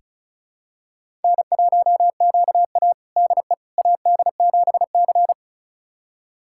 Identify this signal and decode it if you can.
Morse code, heard as a keyed tone, about 36 wpm, 700 Hz. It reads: N1QA DE AD7C
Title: USA Callsign DE Callsign - 4 Characters - 35wpm